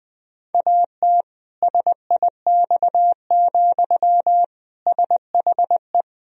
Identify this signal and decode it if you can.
Morse code, heard as a keyed tone, about 21 wpm, 700 Hz. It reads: AT SIX, SHE
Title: Sentences from Top 1000 Words - Review Part 2 - 20wpm